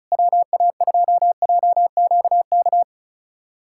Morse code, heard as a keyed tone, about 36 wpm, 700 Hz. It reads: WA2JQK
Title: USA Callsign DE Callsign - 6 Characters - 35wpm